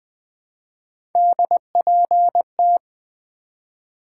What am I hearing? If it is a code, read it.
Morse code, heard as a keyed tone, about 20 wpm, 700 Hz. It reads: DPT
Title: Three Letter-Number Combinations - 20wpm